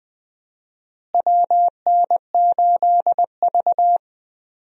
Morse code, heard as a keyed tone, about 20 wpm, 700 Hz. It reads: WN8V
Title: USA Call Signs - 4 Characters - Repeated 2x - 20wpm